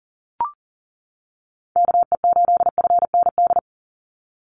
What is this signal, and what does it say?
Morse code, heard as a keyed tone, about 40 wpm, 700 Hz. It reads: KE8FND
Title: USA Call Signs 40wpm